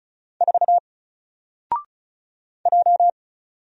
Morse code, heard as a keyed tone, about 36 wpm, 700 Hz. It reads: 4 J
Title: Single Character (Letter:Number:Punctuation:Prosign) 35wpm